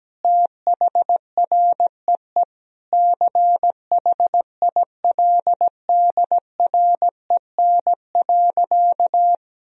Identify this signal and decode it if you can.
Morse code, heard as a keyed tone, about 17 wpm, 700 Hz. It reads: THREE CHILDREN.
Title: Sentences from Top 1000 Words - Review Part 1 - 17wpm